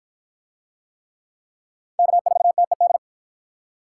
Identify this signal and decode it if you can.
Morse code, heard as a keyed tone, about 48 wpm, 700 Hz. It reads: K4TED